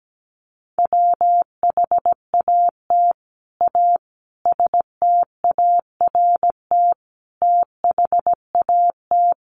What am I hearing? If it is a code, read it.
Morse code, heard as a keyed tone, about 17 wpm, 700 Hz. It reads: WHAT A START THAT
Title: Sentences from Top 300 Words 17wpm